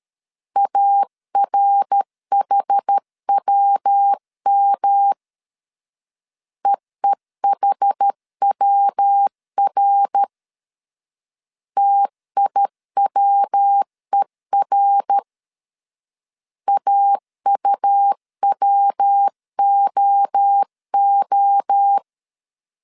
Morse code, heard as a keyed tone, about 13 wpm, 800 Hz. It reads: ARHWM EEHWR TIWER AUWOO
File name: CW / DET / 1 3 1500C S A 12wpm 2905